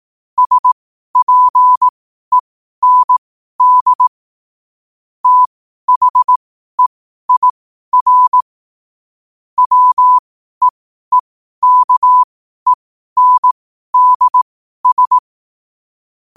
Morse code, heard as a keyed tone, about 18 wpm, spaced as for 12 wpm, 1000 Hz. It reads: SPEND THEIR WEEKENDS